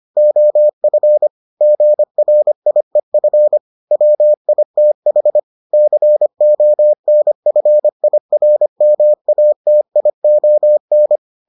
Morse code, heard as a keyed tone, about 25 wpm, 600 Hz. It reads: OF GRIEF WITH CONFIRMATION